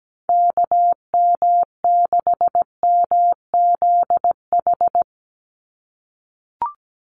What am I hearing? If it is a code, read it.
Morse code, heard as a keyed tone, about 17 wpm, 700 Hz. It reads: KM6MZH
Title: USA Call Signs 17wpm